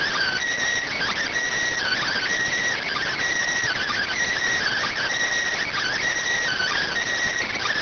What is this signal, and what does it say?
Emergency Alert System (SAME) tones (header bursts), heard as harsh buzzing bursts. It is heard from 1.2 to 7.4 s.